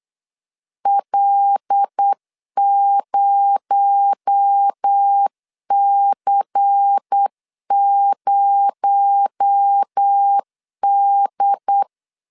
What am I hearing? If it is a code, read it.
Morse code, heard as a keyed tone, about 8 wpm, 800 Hz. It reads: L0C0D
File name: LEZIONI / 6 PKX0 / 6 PKX0 8wpm 3341